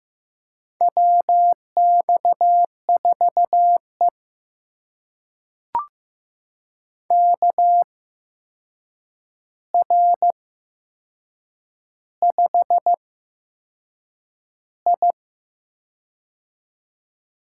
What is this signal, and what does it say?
Morse code, heard as a keyed tone, about 15 wpm, spaced as for 6 wpm, 700 Hz. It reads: WX4E KR5I
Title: USA Call Signs - 4 Characters - 8x Character Spacing - 15wpm